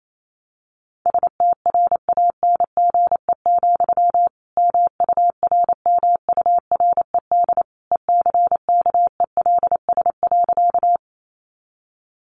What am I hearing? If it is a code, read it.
Morse code, heard as a keyed tone, about 28 wpm, 700 Hz. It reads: STRANGE, MURMURED ECKELS.
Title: A Sound Of Thunder 28wpm